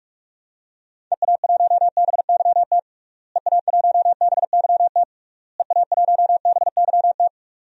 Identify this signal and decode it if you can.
Morse code, heard as a keyed tone, about 46 wpm, 700 Hz. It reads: EA1BYT EA1BYT EA1BYT
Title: DXCC-100 Call Signs - 6 Characters - Repeated 3x - 45wpm